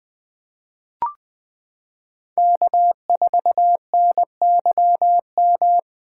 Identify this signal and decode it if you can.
Morse code, heard as a keyed tone, about 20 wpm, 700 Hz. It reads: K4NYM
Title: 2023 POTA Activator Call Signs - 20wpm